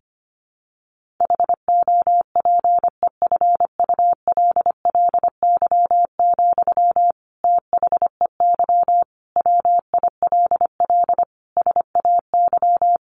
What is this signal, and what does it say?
Morse code, heard as a keyed tone, about 25 wpm, 700 Hz. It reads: HOPEFULLY, THEY WILL SAY
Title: Sentences from Top 1500 Words 25wpm